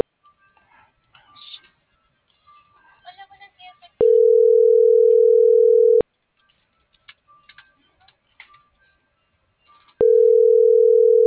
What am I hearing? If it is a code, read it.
Ringback tone (North America), 440 + 480 Hz.